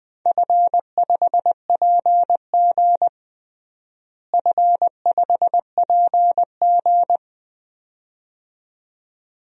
Morse code, heard as a keyed tone, about 20 wpm, 700 Hz. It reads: F5PG F5PG
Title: DXCC-100 Call Signs - 4 Characters - Repeated 2x - 20wpm